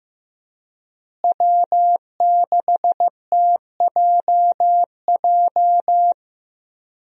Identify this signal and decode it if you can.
Morse code, heard as a keyed tone, about 14 wpm, 700 Hz. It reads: W6TJJ
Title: USA Call Signs - 5 Characters - Repeated 3x - 15wpm